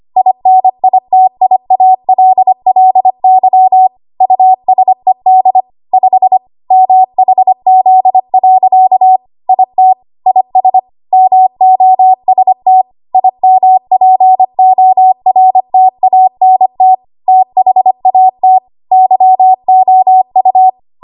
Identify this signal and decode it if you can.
Morse code, heard as a keyed tone, about 25 wpm, 750 Hz. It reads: INITIALLY USED 5 MHZ. IT IS MOST IMPORTANT THAT YOU
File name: Archive / 25 WPM / 25WPM